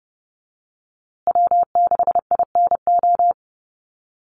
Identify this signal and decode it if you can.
Morse code, heard as a keyed tone, about 30 wpm, 700 Hz. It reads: W6INO
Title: USA Call Signs Repeated 2x 30wpm